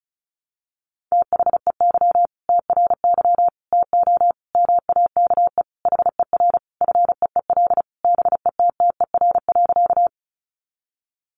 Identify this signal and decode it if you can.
Morse code, heard as a keyed tone, about 34 wpm, 700 Hz. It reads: THEY TRY TO MAKE HER FEEL BETTER.
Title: Easy Phrases 35wpm